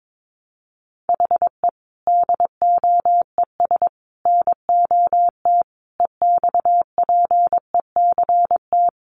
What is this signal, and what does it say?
Morse code, heard as a keyed tone, about 22 wpm, 700 Hz. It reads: HE DOES NOT EXPECT
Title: Sentences from Top 1100 Words 22wpm